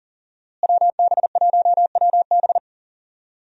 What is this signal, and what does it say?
Morse code, heard as a keyed tone, about 38 wpm, 700 Hz. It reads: WB1WB
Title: USA Call Signs - 5 Characters - Repeated 2x - 40wpm